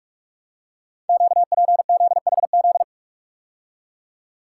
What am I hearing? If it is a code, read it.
Morse code, heard as a keyed tone, about 44 wpm, 700 Hz. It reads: QPZHZ
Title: Code Groups - 5 Letters - 45wpm